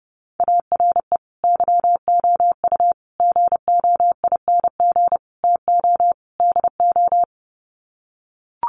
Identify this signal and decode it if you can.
Morse code, heard as a keyed tone, about 30 wpm, 700 Hz. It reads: ARE YOU GOING TO DO
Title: Sets of 5 Words 30wpm